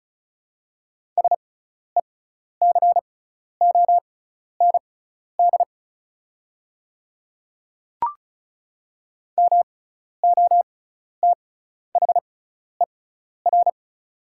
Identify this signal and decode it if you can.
Morse code, heard as a keyed tone, about 34 wpm, spaced as for 12 wpm, 700 Hz. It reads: SECOND MOTHER
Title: 6x Character Spacing - Top 300 Words - 35wpm